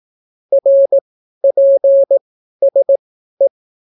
Morse code, heard as a keyed tone, about 18 wpm, spaced as for 12 wpm, 550 Hz. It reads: RPSE